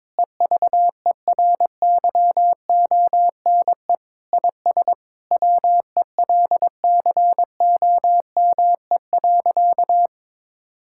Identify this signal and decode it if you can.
Morse code, heard as a keyed tone, about 22 wpm, 700 Hz. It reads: EVERYONE IS WELCOME.
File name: Sentences with 7-Word Phrases - 22wpm